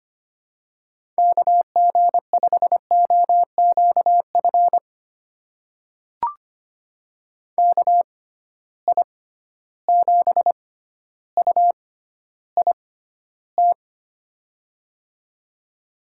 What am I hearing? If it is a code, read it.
Morse code, heard as a keyed tone, about 25 wpm, spaced as for 13 wpm, 700 Hz. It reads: KG5OQF KI7UIT
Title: USA Call Signs - 6 Characters - 6x Character Spacing - 25wpm